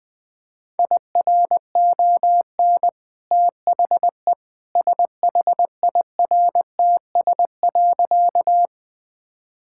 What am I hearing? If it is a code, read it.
Morse code, heard as a keyed tone, about 20 wpm, 700 Hz. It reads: IRON THE SHIRTS.
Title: Easy Sentences with Hints - 20wpm